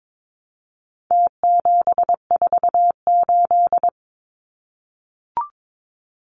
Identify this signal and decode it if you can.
Morse code, heard as a keyed tone, about 22 wpm, 700 Hz. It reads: T748
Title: Numbers - Contest Style - 4-Digit 22wpm